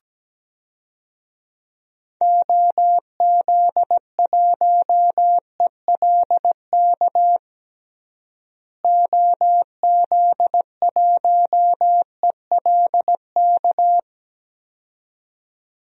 Morse code, heard as a keyed tone, about 17 wpm, 700 Hz. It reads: OZ1ELK OZ1ELK
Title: DXCC-100 Call Signs - 6 Characters - Repeated 2x - 17wpm